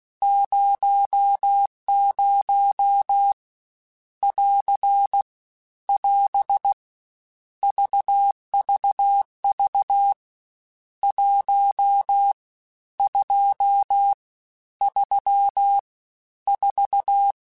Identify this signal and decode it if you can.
Morse code, heard as a keyed tone, about 16 wpm, 800 Hz. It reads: 00 <AR> <AS> VVV 1 2 3 4
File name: Misc / Misc-mp3s / Numb @15